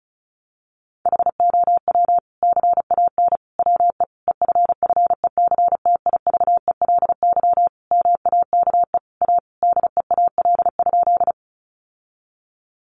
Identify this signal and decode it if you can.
Morse code, heard as a keyed tone, about 34 wpm, 700 Hz. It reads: HOW CAN WE EFFECTIVELY MAKE A DEAL?
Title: Sentences from Top 1400 Words 35wpm